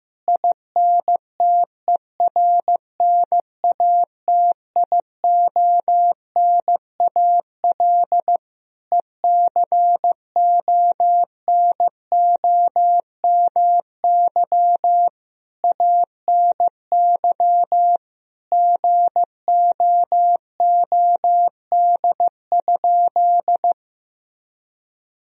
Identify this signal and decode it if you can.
Morse code, heard as a keyed tone, about 15 wpm, 700 Hz. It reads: INTERNATIONAL ECONOMY ANY GOOD?
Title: Sentences from Top 1000 Words - Review Part 4 - 15wpm